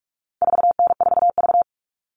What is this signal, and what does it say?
Morse code, heard as a keyed tone, about 44 wpm, 700 Hz. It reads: 4N4V
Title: All Cut Numbers - 4-Digit 45wpm